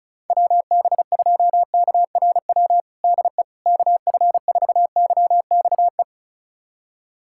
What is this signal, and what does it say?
Morse code, heard as a keyed tone, about 36 wpm, 700 Hz. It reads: WB2KRW DE KF4YXE
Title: USA Callsign DE Callsign - 6 Characters - 35wpm